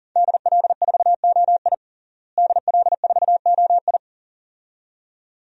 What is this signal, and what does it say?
Morse code, heard as a keyed tone, about 37 wpm, 700 Hz. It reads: DL4OI DL4OI
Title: USA Call Signs - 5 Characters - Repeated 2x - 40wpm